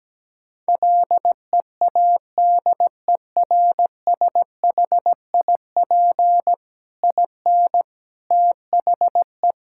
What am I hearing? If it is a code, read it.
Morse code, heard as a keyed tone, about 17 wpm, 700 Hz. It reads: LEADERSHIP IN THE
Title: Sentences from Top 1600 Words 17wpm